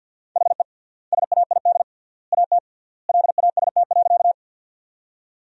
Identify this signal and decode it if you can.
Morse code, heard as a keyed tone, about 52 wpm, 700 Hz. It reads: HE SAID AT LAST.
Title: 3x Word-Spacing - Sentences from Top 300 Words - 50wpm